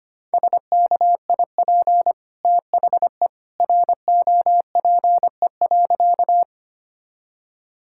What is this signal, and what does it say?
Morse code, heard as a keyed tone, about 25 wpm, 700 Hz. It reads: SKIP THE ROPE.
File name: Easy Sentences with Hints - 25wpm